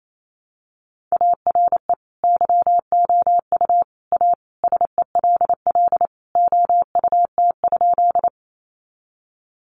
Morse code, heard as a keyed tone, about 28 wpm, 700 Hz. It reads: ARE YOU A SELL OUT?
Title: Sentences from Top 1100 Words 28wpm